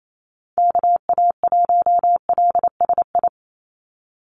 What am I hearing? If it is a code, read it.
Morse code, heard as a keyed tone, about 28 wpm, 700 Hz. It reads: KA1LSI
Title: USA Call Signs 28wpm